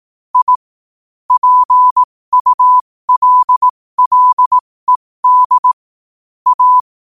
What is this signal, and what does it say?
Morse code, heard as a keyed tone, about 18 wpm, spaced as for 15 wpm, 1000 Hz. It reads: I PULLED A
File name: qotdfiles / 15wpm